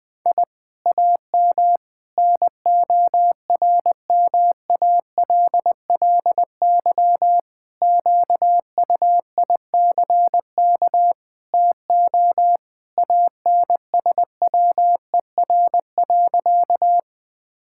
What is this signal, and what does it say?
Morse code, heard as a keyed tone, about 20 wpm, 700 Hz. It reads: I AM NORMALLY QUICK TO ANSWER.
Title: Sentences from Top 1000 Words - Review Part 6 - 20wpm